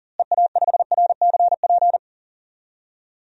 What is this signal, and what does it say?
Morse code, heard as a keyed tone, about 40 wpm, 700 Hz. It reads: EA5RCP
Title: DXCC-100 Call Signs - 6 Characters - Repeated 3x - 40wpm